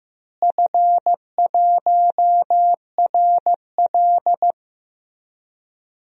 Morse code, heard as a keyed tone, about 15 wpm, 700 Hz. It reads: F1RL
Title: DXCC-100 Call Signs - 4 Characters - Repeated 2x - 15wpm